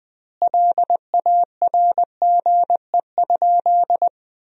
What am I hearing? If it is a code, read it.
Morse code, heard as a keyed tone, about 20 wpm, 700 Hz. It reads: LARGE?
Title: 2x Word-Spacing - Sentences from Top 200 Words - 20wpm